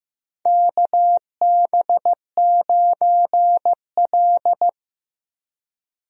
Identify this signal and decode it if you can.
Morse code, heard as a keyed tone, about 15 wpm, 700 Hz. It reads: KB9L
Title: USA Call Signs - 4 Characters - Repeated 2x - 15wpm